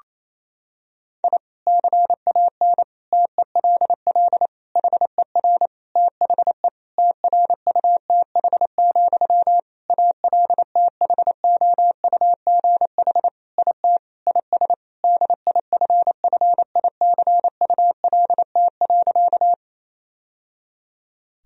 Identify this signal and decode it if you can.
Morse code, heard as a keyed tone, about 28 wpm, 700 Hz. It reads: I CAN TELL HER THE TRUTH, ALTHOUGH IT IS DIFFICULT.
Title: Sentences from Top 1000 Words - Review Part 3 - 28wpm